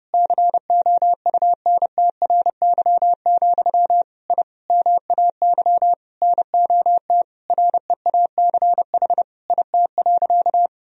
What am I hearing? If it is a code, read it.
Morse code, heard as a keyed tone, about 29 wpm, 700 Hz. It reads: COUNTRY, I MAY NOT REACH IT.
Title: Sentences from Top 1000 Words - Review Part 2 - 30wpm